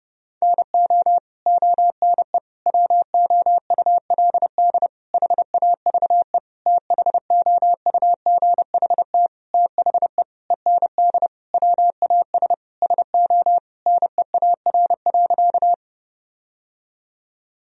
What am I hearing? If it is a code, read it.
Morse code, heard as a keyed tone, about 30 wpm, 700 Hz. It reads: NO ONE WOULD HAVE THOUGHT THE END WAS SO NEAR.
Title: Sentences from Top 1000 Words - Review Part 1 - 30wpm